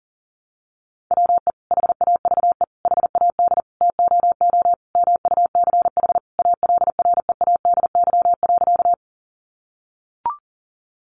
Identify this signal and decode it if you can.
Morse code, heard as a keyed tone, about 37 wpm, 700 Hz. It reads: WE HAVE HAD TOO MUCH ALREADY.
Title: Sentences from Top 700 Words 40wpm